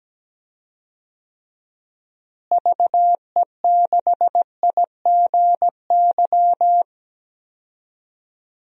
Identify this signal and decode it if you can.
Morse code, heard as a keyed tone, about 17 wpm, 700 Hz. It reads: VE6IGY
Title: DXCC-100 Call Signs - 6 Characters - Repeated 2x - 17wpm